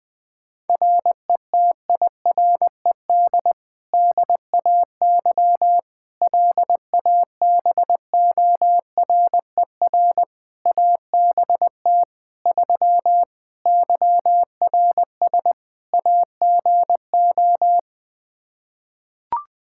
Morse code, heard as a keyed tone, about 21 wpm, 700 Hz. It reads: RETIRED DAY LABORER ABT 3 YRS AGO
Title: Practice QSO - Exchange 3 - More Info - 20wpm